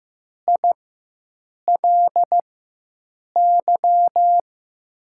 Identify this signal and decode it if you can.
Morse code, heard as a keyed tone, about 16 wpm, spaced as for 7 wpm, 700 Hz. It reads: ILY
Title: USA Call Signs - 6 Characters - 4x Character Spacing - 15wpm